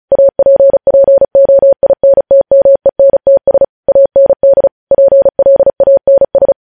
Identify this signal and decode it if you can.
Morse code, heard as a keyed tone, about 36 wpm, 550 Hz. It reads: APPOINTMENTS AND PLANS
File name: podcast / News headlines-1x-35x35